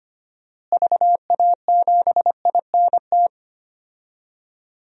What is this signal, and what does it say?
Morse code, heard as a keyed tone, about 25 wpm, 700 Hz. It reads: VA7INT
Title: DXCC-100 Call Signs - 6 Characters - Repeated 3x - 25wpm